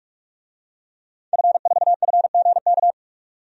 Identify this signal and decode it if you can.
Morse code, heard as a keyed tone, about 44 wpm, 700 Hz. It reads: F4FGK
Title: DXCC-100 Call Signs - 5 Characters - 45wpm